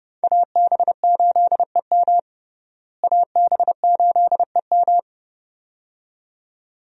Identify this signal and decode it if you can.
Morse code, heard as a keyed tone, about 32 wpm, 700 Hz. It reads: AB8EM AB8EM
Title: USA Call Signs - 5 Characters - Repeated 2x - 30wpm